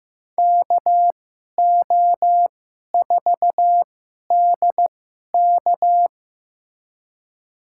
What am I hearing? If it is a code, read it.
Morse code, heard as a keyed tone, about 15 wpm, spaced as for 11 wpm, 700 Hz. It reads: KO4DK
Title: USA Call Signs - 5 Characters - 2x Character Spacing - 15wpm